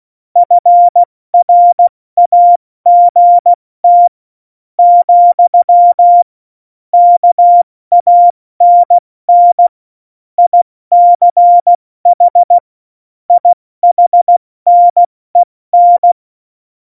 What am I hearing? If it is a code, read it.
Morse code, heard as a keyed tone, about 16 wpm, spaced as for 13 wpm, 700 Hz. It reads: FRAGT, KANN ICH IHNEN
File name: Morsekurs / Witze / Witz 05. Februar 2019, 70 F80